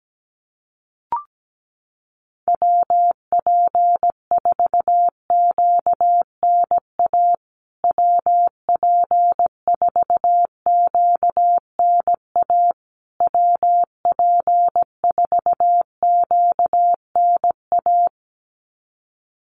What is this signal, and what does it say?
Morse code, heard as a keyed tone, about 17 wpm, 700 Hz. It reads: WP4QNA WP4QNA WP4QNA
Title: USA Call Signs Repeated 3x 17wpm